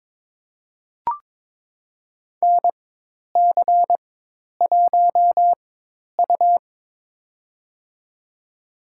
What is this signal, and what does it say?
Morse code, heard as a keyed tone, about 23 wpm, spaced as for 10 wpm, 700 Hz. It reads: NC1U
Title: USA Call Signs - 4 Characters - 4x Character Spacing - 22wpm